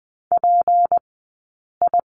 Morse code, heard as a keyed tone, about 20 wpm, spaced as for 8 wpm, 700 Hz. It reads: PI
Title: 204 Lesson 33d - Letter j - Two Characters 20wpm